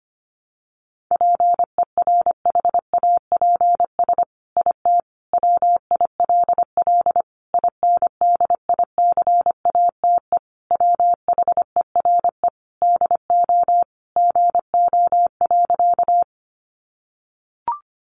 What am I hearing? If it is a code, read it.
Morse code, heard as a keyed tone, about 25 wpm, 700 Hz. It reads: PERHAPS IT WILL INDICATE WHERE DO GO.
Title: Sentences from Top 1300 Words 25wpm